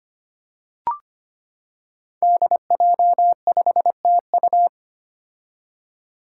Morse code, heard as a keyed tone, about 25 wpm, 700 Hz. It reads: DJ5TU
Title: Five Challenging Letter-Number Combinations 25wpm